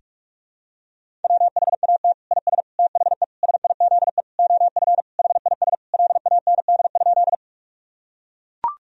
Morse code, heard as a keyed tone, about 41 wpm, 700 Hz. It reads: WHAT IS THE SIZE OF HIS LAND?
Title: Sentences from Top 1000 Words - Review Part 2 - 45wpm